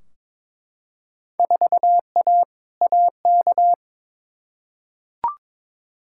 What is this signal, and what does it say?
Morse code, heard as a keyed tone, about 22 wpm, 700 Hz. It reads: 4A AK
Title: Field Day Exchanges - 22wpm